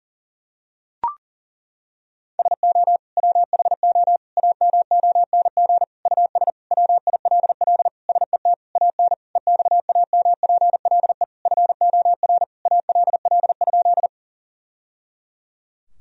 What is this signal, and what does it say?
Morse code, heard as a keyed tone, about 40 wpm, 700 Hz. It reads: SO WHO AMONG US WILL SET AN EXAMPLE FOR ALL?
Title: Sentences from Top 1000 Words - Review Part 3 - 40wpm